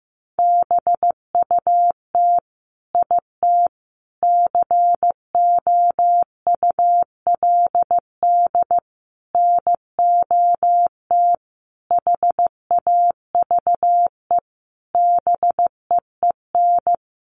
Morse code, heard as a keyed tone, about 14 wpm, 700 Hz. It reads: BUT IT COULD NOT HAVE BEEN
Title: Sentences from Top 1400 Words 15wpm